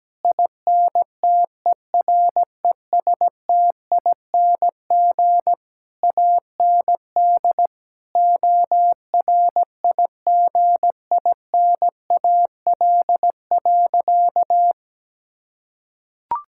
Morse code, heard as a keyed tone, about 17 wpm, 700 Hz. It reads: INTERESTING AND ORIGINAL.
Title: Sentences from Top 1000 Words - Review Part 6 - 17wpm